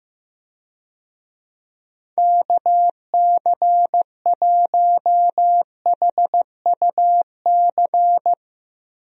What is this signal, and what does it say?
Morse code, heard as a keyed tone, about 15 wpm, 700 Hz. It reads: KC1HUC
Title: USA Call Signs - 6 Characters - Repeated 2x - 15wpm